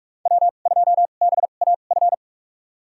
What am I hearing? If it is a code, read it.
Morse code, heard as a keyed tone, about 47 wpm, spaced as for 33 wpm, 700 Hz. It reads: W2BAF
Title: USA Call Signs - 2x Character Spacing - 45wpm